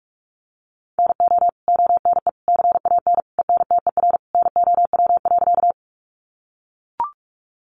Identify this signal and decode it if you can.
Morse code, heard as a keyed tone, about 44 wpm, 700 Hz. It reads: NO ONE CAN ENTER NOW.
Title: Sentences from Top 1300 Words 45wpm